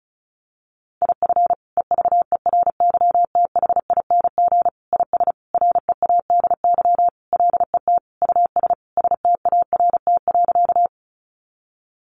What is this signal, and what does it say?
Morse code, heard as a keyed tone, about 34 wpm, 700 Hz. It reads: IF EVERYTHING IS READY LET US START.
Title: Sentences from Top 700 Words 35wpm